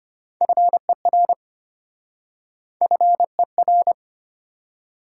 Morse code, heard as a keyed tone, about 33 wpm, 700 Hz. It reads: FER FER